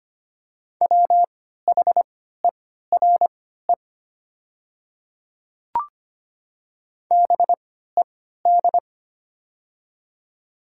Morse code, heard as a keyed tone, about 25 wpm, spaced as for 14 wpm, 700 Hz. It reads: WHERE BED
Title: 3x Character Spacing - Top 500 Words - 25wpm